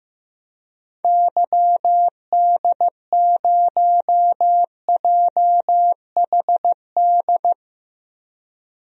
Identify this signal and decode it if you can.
Morse code, heard as a keyed tone, about 16 wpm, 700 Hz. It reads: YD0JHD
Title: DXCC-100 Call Signs - 6 Characters - Repeated 2x - 15wpm